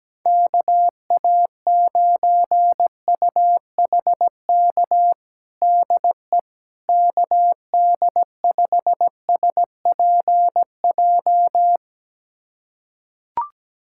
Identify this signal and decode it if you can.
Morse code, heard as a keyed tone, about 17 wpm, 700 Hz. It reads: KA9UHK DE KD5SPJ
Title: USA Callsign DE Callsign - 6 Characters - 17wpm